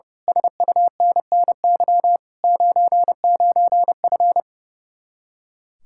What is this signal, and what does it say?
Morse code, heard as a keyed tone, about 30 wpm, 700 Hz. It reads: SUNNY 99F
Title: Speed-Racing Common QSO Phrases - 30wpm